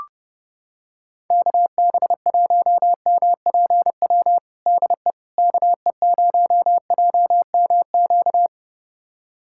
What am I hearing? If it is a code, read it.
Morse code, heard as a keyed tone, about 30 wpm, 700 Hz. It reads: KB1MPW DE KE0JMQ
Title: USA Callsign DE Callsign - 6 Characters - 30wpm